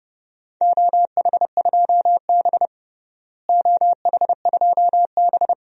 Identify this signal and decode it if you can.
Morse code, heard as a keyed tone, about 30 wpm, 700 Hz. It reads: OH2B OH2B
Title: DXCC-100 Call Signs - 4 Characters - Repeated 2x - 30wpm